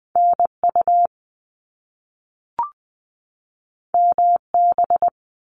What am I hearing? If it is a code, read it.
Morse code, heard as a keyed tone, about 20 wpm, 700 Hz. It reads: NU MB
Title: Canadian Provinces and Territories 20wpm